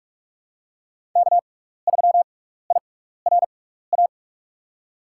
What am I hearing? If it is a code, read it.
Morse code, heard as a keyed tone, about 44 wpm, spaced as for 15 wpm, 700 Hz. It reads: K3IRA
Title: USA Call Signs - 5 Characters - 6x Character Spacing - 45wpm